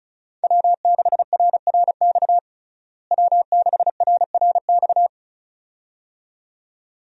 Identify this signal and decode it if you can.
Morse code, heard as a keyed tone, about 36 wpm, 700 Hz. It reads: W6RRX W6RRX
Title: USA Call Signs - 5 Characters - Repeated 2x - 35wpm